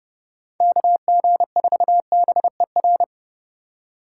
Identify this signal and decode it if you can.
Morse code, heard as a keyed tone, about 29 wpm, 700 Hz. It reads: KG4BER